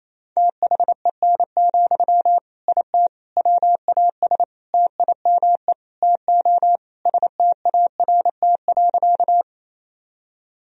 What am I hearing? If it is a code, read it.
Morse code, heard as a keyed tone, about 28 wpm, 700 Hz. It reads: THEN, IT WAS TIME TO START.
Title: Sentences from Top 1000 Words - Review Part 1 - 28wpm